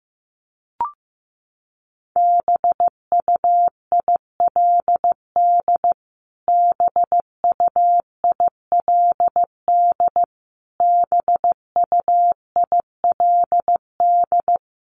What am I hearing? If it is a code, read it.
Morse code, heard as a keyed tone, about 15 wpm, 700 Hz. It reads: BUILD BUILD BUILD
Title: Top 200 Words Repeated 15wpm